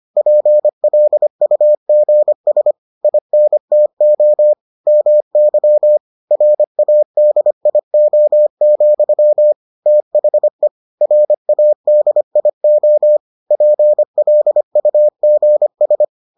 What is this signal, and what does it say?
Morse code, heard as a keyed tone, about 25 wpm, 600 Hz. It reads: PLUGS INTO MY RADIO, THE RADIO PLUGS